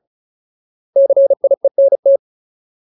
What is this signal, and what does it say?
Morse code, heard as a keyed tone, about 37 wpm, 550 Hz. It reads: CIENT
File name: Ultimate Word List / 35WPM Ultimate